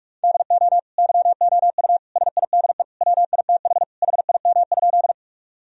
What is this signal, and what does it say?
Morse code, heard as a keyed tone, about 46 wpm, 700 Hz. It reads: DO YOU SIDE WITH HIM?
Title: Sentences from Top 1000 Words - Review Part 1 - 45wpm